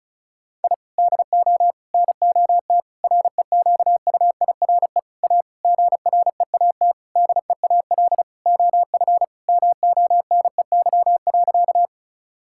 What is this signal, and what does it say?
Morse code, heard as a keyed tone, about 36 wpm, 700 Hz. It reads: I DO NOT REQUIRE A GREAT DEAL OF MONEY.
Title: Sentences from Top 1000 Words - Review Part 6 - 35wpm